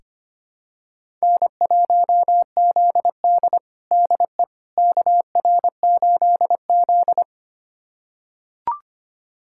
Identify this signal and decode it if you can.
Morse code, heard as a keyed tone, about 25 wpm, 700 Hz. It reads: N1ZD DE KR8Z
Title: USA Callsign DE Callsign - 4 Characters - 25wpm